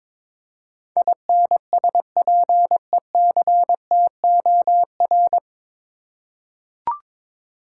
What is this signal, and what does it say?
Morse code, heard as a keyed tone, about 22 wpm, 700 Hz. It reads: INSPECTOR
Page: Morse Code Ninja practice 22wpm